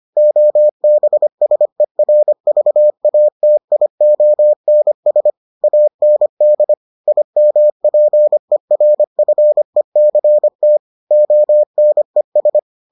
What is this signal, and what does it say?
Morse code, heard as a keyed tone, about 25 wpm, 600 Hz. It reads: OBSERVATIONS AND IMPERFECT ONES